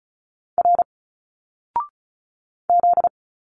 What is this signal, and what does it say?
Morse code, heard as a keyed tone, about 36 wpm, 700 Hz. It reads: R Z
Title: Single Letters 35wpm